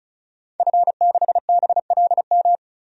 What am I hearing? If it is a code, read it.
Morse code, heard as a keyed tone, about 36 wpm, 700 Hz. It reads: F6BLM
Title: DXCC-100 Call Signs - 5 Characters - 35wpm